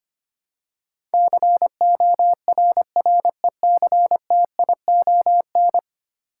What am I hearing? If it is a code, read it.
Morse code, heard as a keyed tone, about 25 wpm, 700 Hz. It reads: CORRECTION
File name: 10 Letter Words - First 3 Letters Given - 25wpm